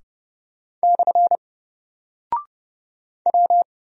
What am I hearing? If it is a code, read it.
Morse code, heard as a keyed tone, about 30 wpm, 700 Hz. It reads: / W
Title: Single Character (Letter:Number:Punctuation:Prosign) Alt 30wpm